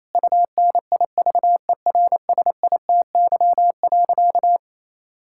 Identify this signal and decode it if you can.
Morse code, heard as a keyed tone, about 27 wpm, 700 Hz. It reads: UNIVERSITY.
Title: Sentences from Top 1000 Words - Review Part 6 - 28wpm